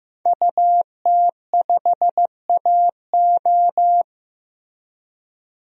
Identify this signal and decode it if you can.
Morse code, heard as a keyed tone, about 15 wpm, 700 Hz. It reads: UT5AO
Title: DXCC-100 Call Signs - 5 Characters - Repeated 3x - 15wpm